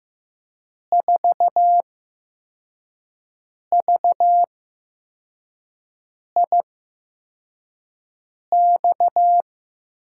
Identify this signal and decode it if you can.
Morse code, heard as a keyed tone, about 15 wpm, spaced as for 4 wpm, 700 Hz. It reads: 4VIX
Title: USA Call Signs - 6 Characters - 8x Character Spacing - 15wpm